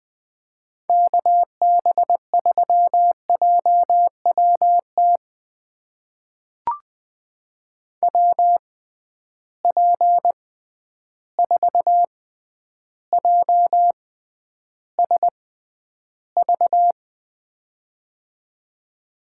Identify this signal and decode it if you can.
Morse code, heard as a keyed tone, about 21 wpm, spaced as for 10 wpm, 700 Hz. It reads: KB3JWT WP4JSV
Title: USA Call Signs - 6 Characters - 6x Character Spacing - 20wpm